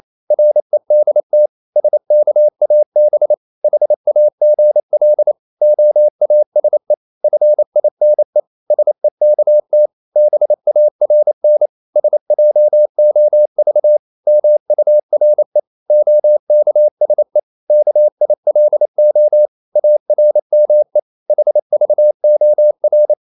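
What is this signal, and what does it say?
Morse code, heard as a keyed tone, about 28 wpm, 600 Hz. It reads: REDT SKAB HAGL OASE FINE SEKT BARN SJOV MURE OKSE KILO ARME HVOR
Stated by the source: Danish short words sent in Morse at 28 wpm, from CW med Gnister.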